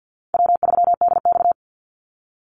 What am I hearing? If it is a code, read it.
Morse code, heard as a keyed tone, about 48 wpm, 700 Hz. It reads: W3DX
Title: NA SOTA Activator Callsigns 50wpm